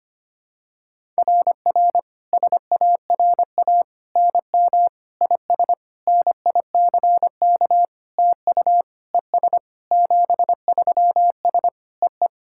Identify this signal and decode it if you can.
Morse code, heard as a keyed tone, about 25 wpm, 700 Hz. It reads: RR SARA NM IS NICK TU ES 73S EE
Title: SOTA QSO Practice 25wpm